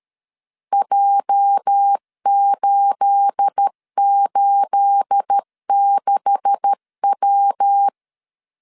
Morse code, heard as a keyed tone, about 13 wpm, 800 Hz. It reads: J886W
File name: CW / DET / 1 9 1500C S A 12wpm 3800